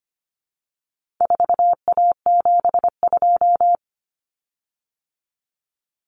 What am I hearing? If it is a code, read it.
Morse code, heard as a keyed tone, about 25 wpm, 700 Hz. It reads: 4A72
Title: Common Cut Numbers - 4-Digit 25wpm